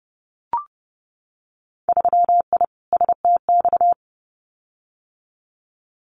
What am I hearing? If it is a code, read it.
Morse code, heard as a keyed tone, about 30 wpm, 700 Hz. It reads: 3I STX
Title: Winter Field Day Exchanges Encore 30wpm